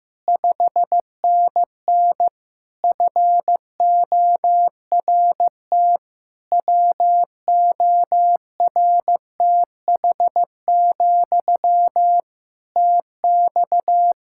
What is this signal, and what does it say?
Morse code, heard as a keyed tone, about 15 wpm, 700 Hz. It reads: 5NN FORT WORTH, TX
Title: Practice QSO - Exchange 1 - Beginning - 15wpm